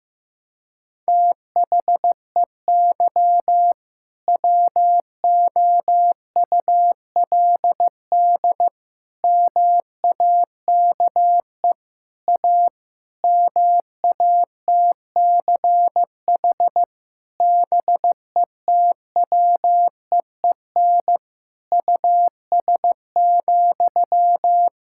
Morse code, heard as a keyed tone, about 16 wpm, 700 Hz. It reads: THEY WOULD MAKE A MATCH BETWEEN US,
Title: Sentences from Top 1600 Words 15wpm